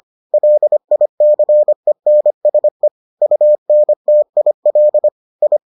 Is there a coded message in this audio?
Yes: Morse code from 0.3 to 5.6 s, heard as a keyed tone, about 25 wpm, 600 Hz. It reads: LICENSE UNTIL I